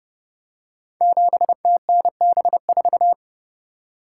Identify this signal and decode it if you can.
Morse code, heard as a keyed tone, about 28 wpm, 700 Hz. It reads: 7TNB4
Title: Code Groups - 5 Letter:Numbers - 30wpm